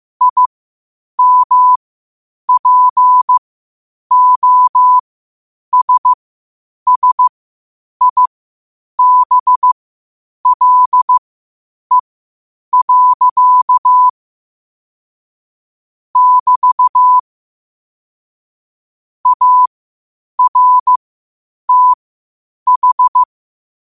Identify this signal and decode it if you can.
Morse code, heard as a keyed tone, about 15 wpm, spaced as for 8 wpm, 1000 Hz. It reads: IMPOSSIBLE. <BT> ARTH
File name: qotdfiles / 8wpm